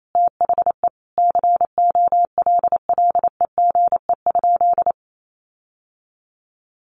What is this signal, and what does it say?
Morse code, heard as a keyed tone, about 28 wpm, 700 Hz. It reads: THE COLLEGE?
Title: Sentences from Top 1300 Words 28wpm